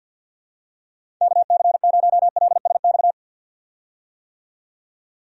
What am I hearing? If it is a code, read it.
Morse code, heard as a keyed tone, about 52 wpm, 700 Hz. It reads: KC0LSX
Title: USA Call Signs - 6 Characters - 50wpm